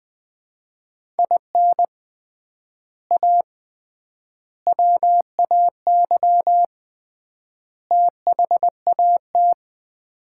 Morse code, heard as a keyed tone, about 20 wpm, 700 Hz. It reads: IN A WAY THAT
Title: 3x Word-Spacing - Sets of 4 Words - 20wpm